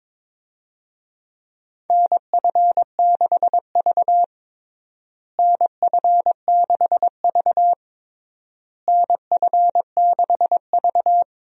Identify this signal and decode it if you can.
Morse code, heard as a keyed tone, about 23 wpm, 700 Hz. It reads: NF6V NF6V NF6V
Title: USA Call Signs - 4 Characters - Repeated 3x - 22wpm